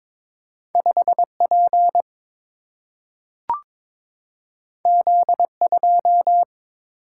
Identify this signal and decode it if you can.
Morse code, heard as a keyed tone, about 22 wpm, 700 Hz. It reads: 5 P Z 2
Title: Single-Letter Single-Number 22wpm